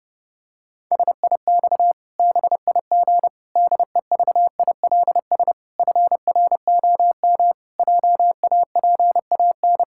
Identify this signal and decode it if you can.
Morse code, heard as a keyed tone, about 32 wpm, 700 Hz. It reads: SIX BIG DEVILS FROM JAPAN
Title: Pangram Challenge v2 - 30wpm